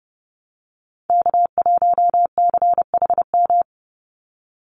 Morse code, heard as a keyed tone, about 30 wpm, 700 Hz. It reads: K1CHM
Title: Call Signs Repeated 2x - 30wpm